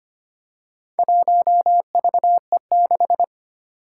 Morse code, heard as a keyed tone, about 25 wpm, 700 Hz. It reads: 1VE6
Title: Four Challenging Letter-Number Combinations 25wpm